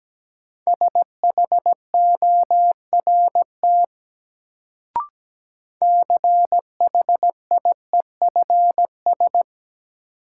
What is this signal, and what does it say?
Morse code, heard as a keyed tone, about 17 wpm, 700 Hz. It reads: SHORT CHIEFS
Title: Sets of 5 Words Encore - New Word Drill - 17wpm